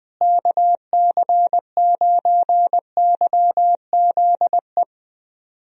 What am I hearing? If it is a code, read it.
Morse code, heard as a keyed tone, about 20 wpm, 700 Hz. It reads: KC9YZE
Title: USA Call Signs - 6 Characters - Repeated 3x - 20wpm